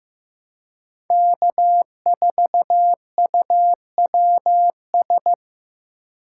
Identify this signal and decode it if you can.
Morse code, heard as a keyed tone, about 16 wpm, 700 Hz. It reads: K4UWS
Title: USA Call Signs - 5 Characters - 4x Character Spacing - 15wpm